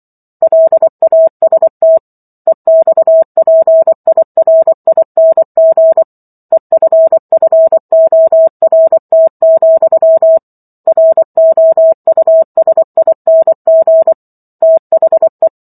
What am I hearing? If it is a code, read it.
Morse code, heard as a keyed tone, about 24 wpm, 650 Hz. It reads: LAST EXPIRING EFFORT, ROUSING THE